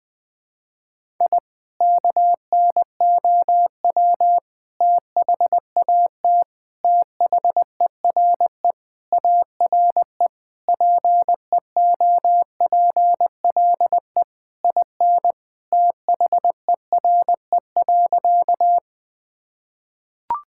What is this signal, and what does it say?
Morse code, heard as a keyed tone, about 21 wpm, 700 Hz. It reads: I KNOW THAT THERE ARE PEOPLE IN THERE.
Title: Sentences with 3-Word Phrases - Spoken Beginning - 20wpm